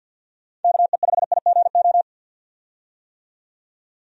Morse code, heard as a keyed tone, about 52 wpm, 700 Hz. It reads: KE5IGO
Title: USA Call Signs - 6 Characters - 50wpm